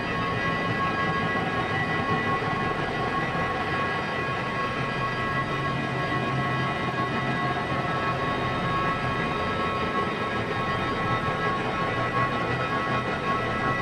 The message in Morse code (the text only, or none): none